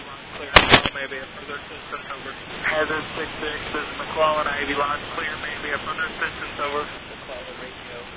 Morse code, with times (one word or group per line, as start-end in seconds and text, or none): none